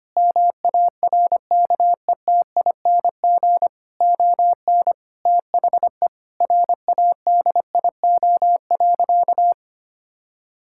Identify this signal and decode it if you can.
Morse code, heard as a keyed tone, about 25 wpm, 700 Hz. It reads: MARKETING ON THE RADIO.
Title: Sentences from Top 1000 Words - Review Part 5 - 25wpm